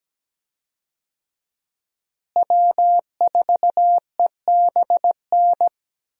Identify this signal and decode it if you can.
Morse code, heard as a keyed tone, about 17 wpm, 700 Hz. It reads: W4EBN
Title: USA Call Signs - 5 Characters - Repeated 2x - 17wpm